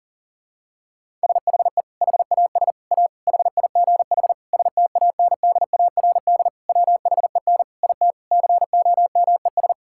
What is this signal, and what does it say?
Morse code, heard as a keyed tone, about 40 wpm, 700 Hz. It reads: SHE HAS A HIGH STANDARD WHEN IT COMES
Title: Sentences from Top 1000 Words - Review Part 6 - 40wpm